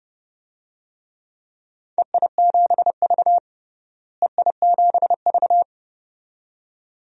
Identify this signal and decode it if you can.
Morse code, heard as a keyed tone, about 31 wpm, 700 Hz. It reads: EI7V EI7V
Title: DXCC-100 Call Signs - 4 Characters - Repeated 2x - 30wpm